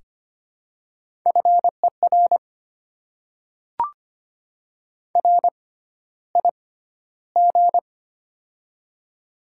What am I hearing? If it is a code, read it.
Morse code, heard as a keyed tone, about 25 wpm, spaced as for 13 wpm, 700 Hz. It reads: FER RIG
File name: Top 20 QSO Elements - 6x Character Spacing - 25wpm